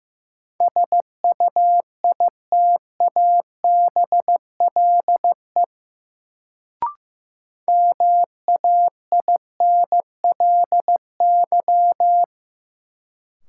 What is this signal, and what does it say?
Morse code, heard as a keyed tone, about 15 wpm, 700 Hz. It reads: SUITABLE MAINLY
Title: Sentences from Top 1400 Words - New Word Drill Alt - 15wpm